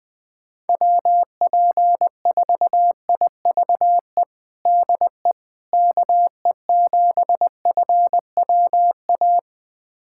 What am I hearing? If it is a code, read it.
Morse code, heard as a keyed tone, about 20 wpm, 700 Hz. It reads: WP4IVE DE KE7FWA
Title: USA Callsign DE Callsign 20wpm